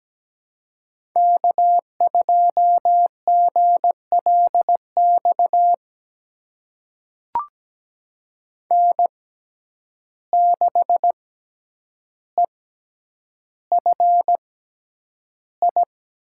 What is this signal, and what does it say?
Morse code, heard as a keyed tone, about 17 wpm, spaced as for 9 wpm, 700 Hz. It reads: K2GLX N6EFI
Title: USA Call Signs - 5 Characters - 6x Character Spacing - 17wpm